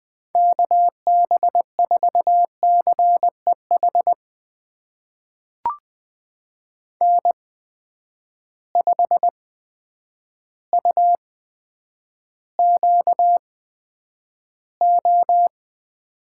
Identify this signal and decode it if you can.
Morse code, heard as a keyed tone, about 20 wpm, spaced as for 9 wpm, 700 Hz. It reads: KB4CEH N5UQO
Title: USA Call Signs - 8x Character Spacing - 20wpm